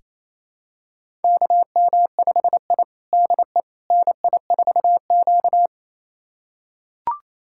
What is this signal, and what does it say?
Morse code, heard as a keyed tone, about 27 wpm, 700 Hz. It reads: KM5I DE NI4Q
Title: USA Callsign DE Callsign - 4 Characters - 28wpm